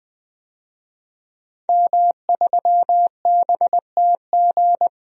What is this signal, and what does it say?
Morse code, heard as a keyed tone, about 21 wpm, 700 Hz. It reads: M3BTG
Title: DXCC-100 Call Signs - 5 Characters - 20wpm